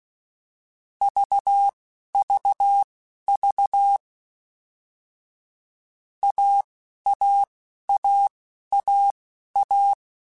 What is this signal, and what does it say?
Morse code, heard as a keyed tone, about 16 wpm, spaced as for 11 wpm, 800 Hz. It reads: VVV AAAAA